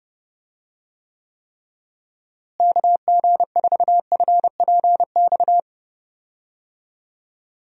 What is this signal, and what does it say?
Morse code, heard as a keyed tone, about 30 wpm, 700 Hz. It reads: KG4FPX
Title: USA Call Signs - 6 Characters - Repeated 3x - 30wpm